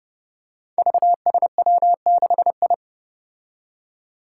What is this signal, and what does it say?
Morse code, heard as a keyed tone, about 30 wpm, 700 Hz. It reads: VSW6I
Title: Five Challenging Letter-Number Combinations 30wpm